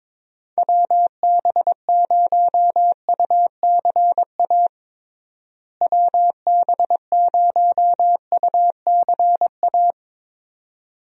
Morse code, heard as a keyed tone, about 22 wpm, 700 Hz. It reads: WB0UCA WB0UCA
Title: USA Call Signs - 6 Characters - Repeated 2x - 22wpm